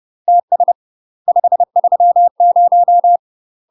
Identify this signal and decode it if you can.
Morse code, heard as a keyed tone, about 30 wpm, 700 Hz. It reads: TS 530
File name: UK MASTER QSOs TRAINING / 30WPM / UK MASTER QSOs 0027